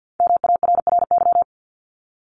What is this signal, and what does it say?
Morse code, heard as a keyed tone, about 50 wpm, 700 Hz. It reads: MARRY
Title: Multi-letter Phonemes Repeated 3x 50wpm